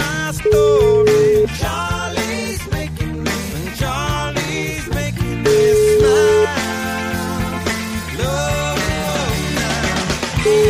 Ringback tone (Europe, 425 Hz), with noise.